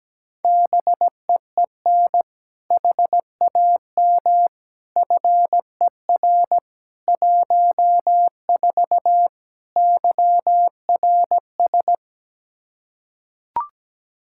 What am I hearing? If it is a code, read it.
Morse code, heard as a keyed tone, about 17 wpm, 700 Hz. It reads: BEEN HAM FER 14 YRS
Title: Practice QSO - Exchange 3 - More Info - 17wpm